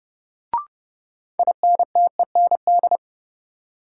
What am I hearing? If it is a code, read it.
Morse code, heard as a keyed tone, about 30 wpm, 700 Hz. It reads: INTEND
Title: Sentences from Top 1600 Words - New Word Drill Alt - 30wpm